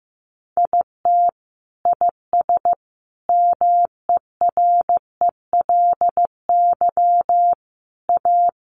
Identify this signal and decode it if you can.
Morse code, heard as a keyed tone, about 15 wpm, 700 Hz. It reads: IT IS MERELY A
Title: Sentences from Top 1400 Words 15wpm